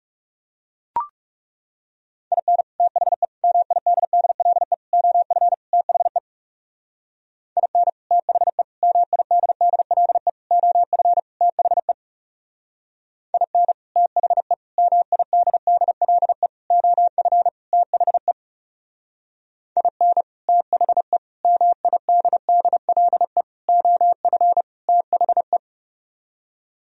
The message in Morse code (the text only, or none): IN THE MIDDLE OF THE IN THE MIDDLE OF THE IN THE MIDDLE OF THE IN THE MIDDLE OF THE